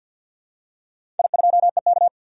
Morse code, heard as a keyed tone, about 51 wpm, 700 Hz. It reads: I2EK
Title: DXCC-100 Call Signs - 4 Characters - 50wpm